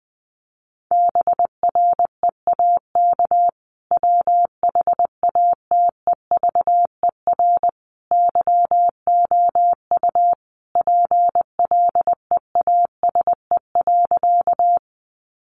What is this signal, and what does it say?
Morse code, heard as a keyed tone, about 20 wpm, 700 Hz. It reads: BREAK WHATEVER YOU PLEASE.
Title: Sentences from Top 1500 Words 20wpm